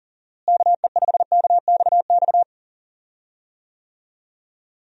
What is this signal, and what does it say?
Morse code, heard as a keyed tone, about 37 wpm, 700 Hz. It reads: KE5KXX
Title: USA Call Signs - 6 Characters - Repeated 2x - 40wpm